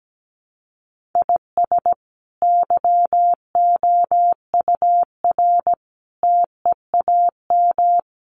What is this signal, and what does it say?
Morse code, heard as a keyed tone, about 17 wpm, 700 Hz. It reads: IS YOUR TEAM
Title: Sentences from Top 1500 Words 17wpm